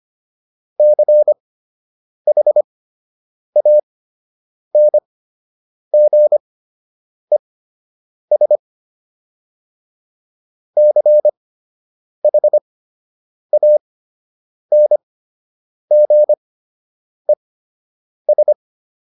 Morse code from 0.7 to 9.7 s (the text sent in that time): CHANGES